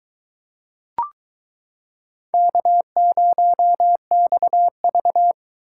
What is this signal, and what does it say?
Morse code, heard as a keyed tone, about 22 wpm, 700 Hz. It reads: K0XV